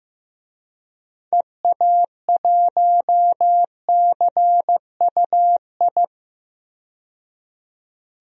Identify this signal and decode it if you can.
Morse code, heard as a keyed tone, about 15 wpm, 700 Hz. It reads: EA1CUI
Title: DXCC-100 Call Signs - 6 Characters - 15wpm